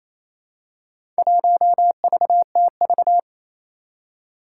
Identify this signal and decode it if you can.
Morse code, heard as a keyed tone, about 29 wpm, 700 Hz. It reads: 1VTV
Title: Four Challenging Letter-Number Combinations 28wpm